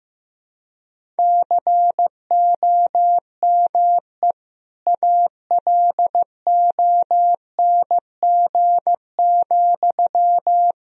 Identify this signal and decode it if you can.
Morse code, heard as a keyed tone, about 16 wpm, 700 Hz. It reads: COME ALONG,
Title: Sentences from Top 1000 Words - Review Part 1 - 15wpm